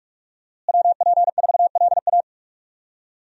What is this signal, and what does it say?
Morse code, heard as a keyed tone, about 41 wpm, 700 Hz. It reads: WP4LA
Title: USA Call Signs - 5 Characters - Repeated 3x - 45wpm